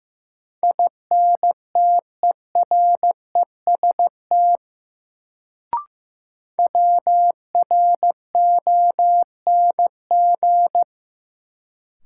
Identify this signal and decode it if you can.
Morse code, heard as a keyed tone, about 15 wpm, 700 Hz. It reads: INTEREST WRONG
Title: Sets of 5 Words Encore - New Word Drill - 15wpm